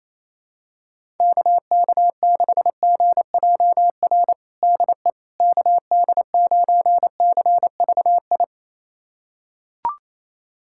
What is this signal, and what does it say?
Morse code, heard as a keyed tone, about 28 wpm, 700 Hz. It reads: KK6GJR DE KD9CVI
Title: USA Callsign DE Callsign - 6 Characters - 28wpm